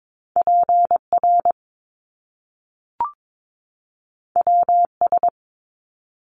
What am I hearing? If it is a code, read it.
Morse code, heard as a keyed tone, about 22 wpm, 700 Hz. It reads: PR WS
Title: Top Two Letter Combinations in Words 22wpm